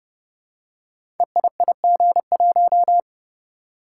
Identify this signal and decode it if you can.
Morse code, heard as a keyed tone, about 29 wpm, 700 Hz. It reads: EIIG1
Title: Code Groups - 5 Letter:Numbers - 30wpm